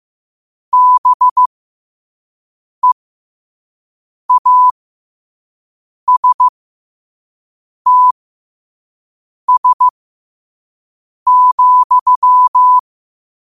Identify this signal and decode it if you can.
Morse code, heard as a keyed tone, about 15 wpm, spaced as for 5 wpm, 1000 Hz. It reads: BEASTS,
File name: qotdfiles / 5wpm